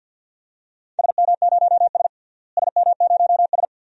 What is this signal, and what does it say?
Morse code, heard as a keyed tone, about 49 wpm, 700 Hz. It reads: SM0S SM0S
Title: DXCC-100 Call Signs - 4 Characters - Repeated 2x - 50wpm